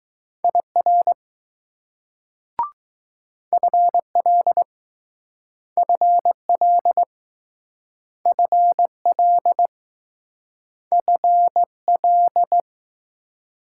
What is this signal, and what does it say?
Morse code, heard as a keyed tone, about 20 wpm, 700 Hz. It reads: IR FL FL FL FL